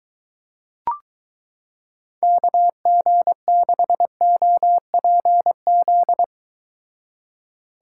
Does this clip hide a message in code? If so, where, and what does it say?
Yes: Morse code from 2.2 to 6.2 s, heard as a keyed tone, about 23 wpm, 700 Hz. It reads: KG6OPZ